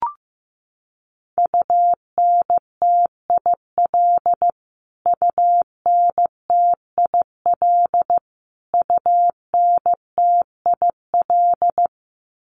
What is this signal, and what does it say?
Morse code, heard as a keyed tone, about 14 wpm, 700 Hz. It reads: UNTIL UNTIL UNTIL
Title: Top 300 Words Repeated 15wpm